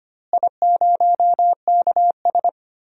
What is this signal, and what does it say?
Morse code, heard as a keyed tone, about 25 wpm, 700 Hz. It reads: I0KS
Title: DXCC-100 Call Signs - 4 Characters - Repeated 2x - 25wpm